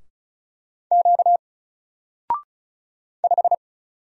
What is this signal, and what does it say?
Morse code, heard as a keyed tone, about 35 wpm, 700 Hz. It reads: Q 5
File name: Single-Letter-Number-35wpm